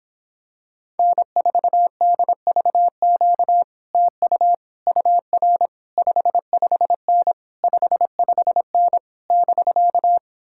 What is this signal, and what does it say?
Morse code, heard as a keyed tone, about 26 wpm, 700 Hz. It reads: N4DVQ TU UR 55N 55N <BK>